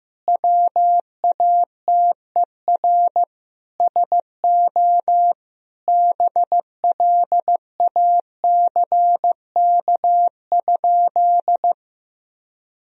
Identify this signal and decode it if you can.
Morse code, heard as a keyed tone, about 16 wpm, 700 Hz. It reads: WATER SO BLACK?
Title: Sentences from Top 1000 Words - Review Part 2 - 15wpm